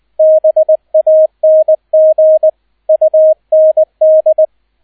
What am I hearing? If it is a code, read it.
Morse code, heard as a keyed tone, about 20 wpm, 630 Hz, with radio noise. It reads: BANG UND
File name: Erster Streich 20wpm